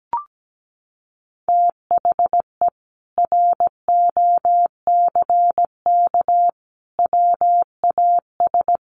Morse code, heard as a keyed tone, about 17 wpm, 700 Hz. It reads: THE ROCK WAS
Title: Sentences from Top 400 Words 17wpm